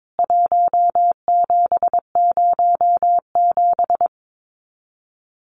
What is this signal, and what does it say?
Morse code, heard as a keyed tone, about 22 wpm, 700 Hz. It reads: 1707
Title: CWT Exchange 22wpm